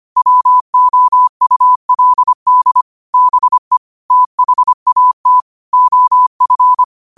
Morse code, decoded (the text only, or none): WOULD BE THAT OF